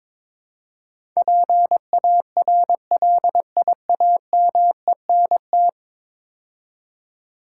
Morse code, heard as a keyed tone, about 22 wpm, 700 Hz. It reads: PARLIAMENT
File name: practice / 22wpm / 10 Letter Words - First 3 Letters Given - 22wpm